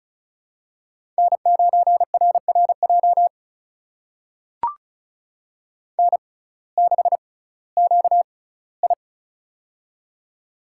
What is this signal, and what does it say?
Morse code, heard as a keyed tone, about 36 wpm, spaced as for 19 wpm, 700 Hz. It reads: N9RRJ N6QI
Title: USA Call Signs - 6x Character Spacing - 35wpm